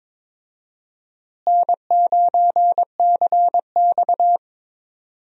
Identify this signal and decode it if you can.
Morse code, heard as a keyed tone, about 21 wpm, 700 Hz. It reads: N9CX
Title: USA Call Signs - 4 Characters - Repeated 2x - 22wpm